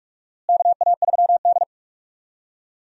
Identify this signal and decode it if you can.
Morse code, heard as a keyed tone, about 46 wpm, 700 Hz. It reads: KA3D
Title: USA Call Signs - 4 Characters - Repeated 2x - 45wpm